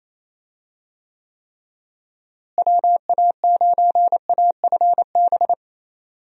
Morse code, heard as a keyed tone, about 27 wpm, 700 Hz. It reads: WA9AFB
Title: USA Call Signs - 6 Characters - Repeated 3x - 28wpm